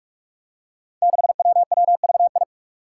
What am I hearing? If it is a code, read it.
Morse code, heard as a keyed tone, about 46 wpm, 700 Hz. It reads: 6WWVI
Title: Five Challenging Letter-Number Combinations 45wpm